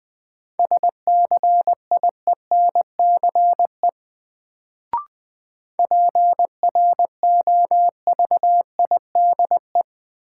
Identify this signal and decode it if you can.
Morse code, heard as a keyed tone, about 20 wpm, 700 Hz. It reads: SCIENCE PROVIDE
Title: Sentences from Top 1000 Words - Review Part 2 - New Word Drill - 20wpm